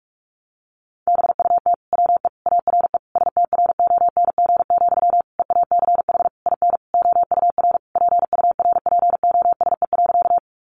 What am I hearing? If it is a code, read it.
Morse code, heard as a keyed tone, about 44 wpm, 700 Hz. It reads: BUT WE ARE STRONG, EACH IN OUR PURPOSE.
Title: Sentences from Top 800 Words 45wpm